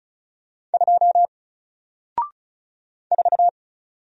Morse code, heard as a keyed tone, about 35 wpm, 700 Hz. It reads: 2 4
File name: Single-Letter-Number-35wpm